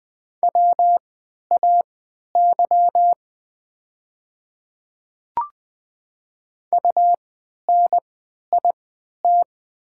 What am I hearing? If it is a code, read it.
Morse code, heard as a keyed tone, about 20 wpm, spaced as for 11 wpm, 700 Hz. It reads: WAY UNIT
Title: 3x Character Spacing - Top 500 Words - 20wpm